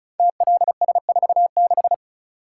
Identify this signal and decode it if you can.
Morse code, heard as a keyed tone, about 36 wpm, 700 Hz. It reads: TLS46
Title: Code Groups - 5 Letter:Numbers - 35wpm